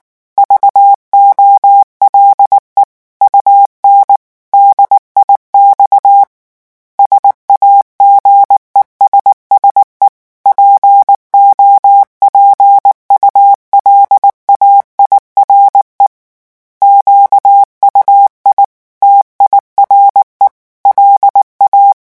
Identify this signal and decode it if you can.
Morse code, heard as a keyed tone, about 19 wpm, 800 Hz. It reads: VOLE UN DIX SAGESSE POPULAIRE QUI TIRE LA
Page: REVISION 8 - 8 mots minute